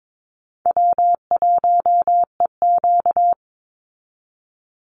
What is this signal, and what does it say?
Morse code, heard as a keyed tone, about 22 wpm, 690 Hz. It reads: W1EQ
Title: Call Signs Repeated 3x - 22wpm